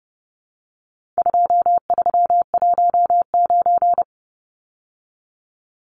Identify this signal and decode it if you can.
Morse code, heard as a keyed tone, about 30 wpm, 700 Hz. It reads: 2319
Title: Numbers - 4-Digit 30wpm